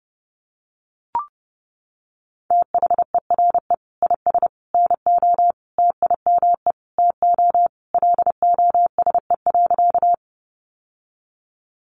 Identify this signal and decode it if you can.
Morse code, heard as a keyed tone, about 30 wpm, 700 Hz. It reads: THERE IS NO TIME TO LOSE.
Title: Sentences from Top 1100 Words 30wpm